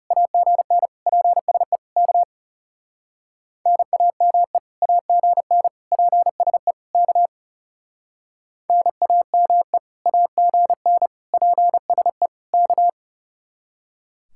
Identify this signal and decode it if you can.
Morse code, heard as a keyed tone, about 37 wpm, 700 Hz. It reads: AGN PSE K NAME AGN PSE K NAME AGN PSE K
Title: Speed-Racing Common QSO Phrases - 30wpm